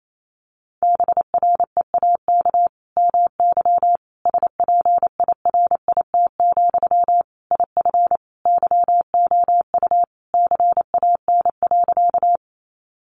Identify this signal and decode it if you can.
Morse code, heard as a keyed tone, about 28 wpm, 700 Hz. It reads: BREAK MY SPIRIT, IF YOU CAN.
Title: Sentences from Top 1400 Words 28wpm